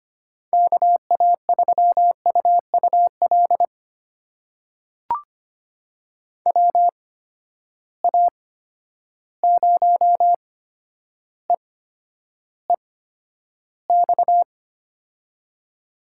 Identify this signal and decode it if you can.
Morse code, heard as a keyed tone, about 25 wpm, spaced as for 11 wpm, 700 Hz. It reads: KA3UUL WA0EEX
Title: USA Call Signs - 6 Characters - 8x Character Spacing - 25wpm